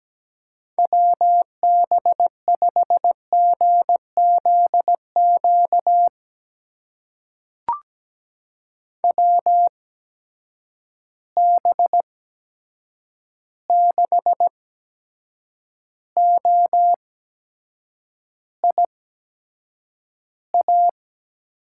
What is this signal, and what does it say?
Morse code, heard as a keyed tone, about 17 wpm, spaced as for 7 wpm, 700 Hz. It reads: WB5GZQ WB6OIA
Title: USA Call Signs - 6 Characters - 8x Character Spacing - 17wpm